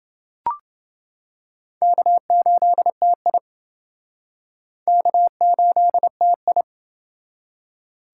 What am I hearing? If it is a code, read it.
Morse code, heard as a keyed tone, about 28 wpm, 700 Hz. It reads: K8TI K8TI